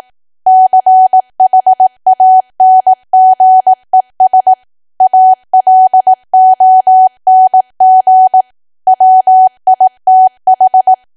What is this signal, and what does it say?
Morse code, heard as a keyed tone, about 18 wpm, 750 Hz. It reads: CHANGES ALONG WITH
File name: cwa / W1AW-Level2 / 18WPM